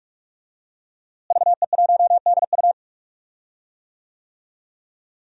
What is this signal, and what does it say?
Morse code, heard as a keyed tone, about 48 wpm, 700 Hz. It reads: VE1DU
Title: DXCC-100 Call Signs - 5 Characters - Repeated 2x - 45wpm